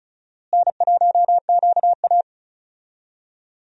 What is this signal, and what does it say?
Morse code, heard as a keyed tone, about 36 wpm, 700 Hz. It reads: N1QA
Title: USA Call Signs - 4 Characters - Repeated 2x - 35wpm